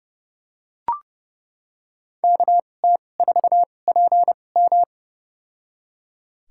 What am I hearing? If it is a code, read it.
Morse code, heard as a keyed tone, about 30 wpm, spaced as for 22 wpm, 700 Hz. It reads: KT4PM
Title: USA Call Signs - 5 Characters - 2x Character Spacing - 30wpm